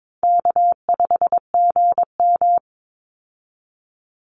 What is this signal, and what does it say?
Morse code, heard as a keyed tone, about 22 wpm, 700 Hz. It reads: K5GM
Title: Call Signs Repeated 3x - 22wpm